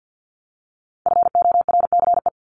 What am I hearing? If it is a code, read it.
Morse code, heard as a keyed tone, about 50 wpm, 700 Hz. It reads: FORCE
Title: Top 500 Words Repeated 50wpm